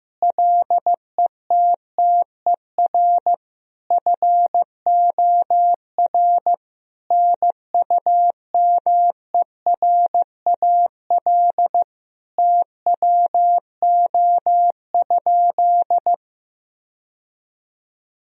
Morse code, heard as a keyed tone, about 16 wpm, 700 Hz. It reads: LETTER FOR NUMERAL TWO?
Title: Sentences from Top 1000 Words - Review Part 2 - 15wpm